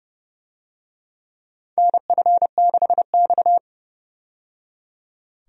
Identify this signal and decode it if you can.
Morse code, heard as a keyed tone, about 30 wpm, 700 Hz. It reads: NF6X
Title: USA Call Signs - 4 Characters - 30wpm